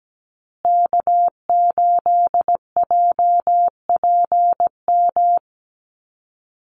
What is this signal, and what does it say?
Morse code, heard as a keyed tone, about 17 wpm, 700 Hz. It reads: K8JPM
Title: Call Signs Repeated 2x - 17wpm